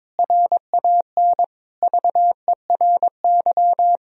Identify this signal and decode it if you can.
Morse code, heard as a keyed tone, about 22 wpm, 700 Hz. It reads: RAN VERY
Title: Sentences from Top 1000 Words - Review Part 3 - 22wpm